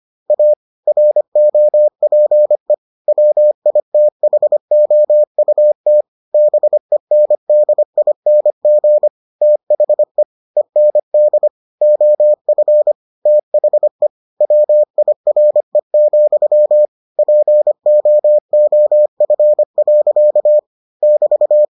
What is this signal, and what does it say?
Morse code, heard as a keyed tone, about 25 wpm, 600 Hz. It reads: A ROPE WITHOUT BENDING THE END OF THE WIRE, POOF. <BT>